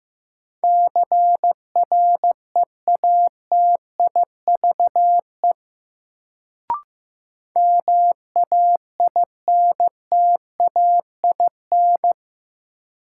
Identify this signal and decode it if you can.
Morse code, heard as a keyed tone, about 15 wpm, 700 Hz. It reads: CREATIVE MAINTAIN
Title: Sentences from Top 1000 Words - Review Part 5 - New Word Drill - 15wpm